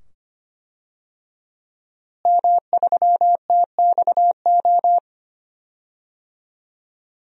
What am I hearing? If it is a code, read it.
Morse code, heard as a keyed tone, about 25 wpm, 700 Hz. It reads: M3TXO
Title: DXCC-100 Call Signs - 5 Characters - Repeated 2x - 25wpm